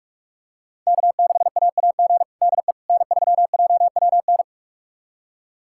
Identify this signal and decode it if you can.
Morse code, heard as a keyed tone, about 46 wpm, 700 Hz. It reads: K6AAG DE N3JWN
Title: USA Callsign DE Callsign - 5 Characters - 45wpm